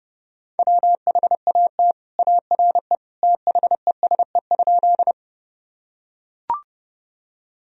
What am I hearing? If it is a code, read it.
Morse code, heard as a keyed tone, about 32 wpm, 700 Hz. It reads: WHAT ARE THESE?
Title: Sentences from Top 1000 Words - Review Part 1 - 30wpm